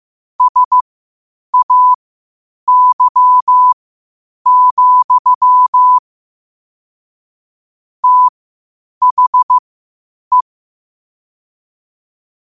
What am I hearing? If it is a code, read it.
Morse code, heard as a keyed tone, about 15 wpm, spaced as for 8 wpm, 1000 Hz. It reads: SAY, THE
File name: qotdfiles / 8wpm